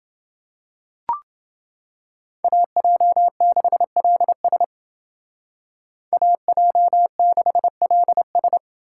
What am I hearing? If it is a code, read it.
Morse code, heard as a keyed tone, about 28 wpm, 700 Hz. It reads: AJ6LS AJ6LS